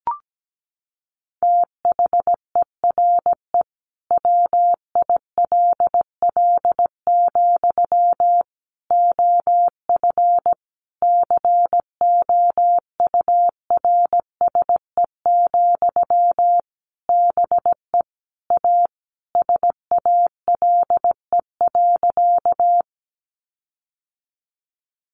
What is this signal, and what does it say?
Morse code, heard as a keyed tone, about 17 wpm, 700 Hz. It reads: THERE WILL, OF COURSE, BE A SALE.
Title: Sentences from Top 1400 Words 17wpm